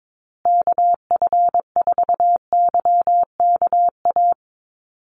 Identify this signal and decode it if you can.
Morse code, heard as a keyed tone, about 22 wpm, 700 Hz. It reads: KF4YKA
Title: USA Call Signs Repeated 2x 22wpm